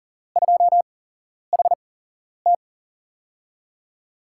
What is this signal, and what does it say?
Morse code, heard as a keyed tone, about 40 wpm, spaced as for 11 wpm, 700 Hz. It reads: 2HT
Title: USA Call Signs - 8x Character Spacing - 40wpm